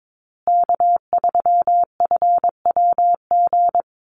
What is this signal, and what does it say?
Morse code, heard as a keyed tone, about 22 wpm, 700 Hz. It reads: K3FWG
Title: USA Call Signs Repeated 3x 22wpm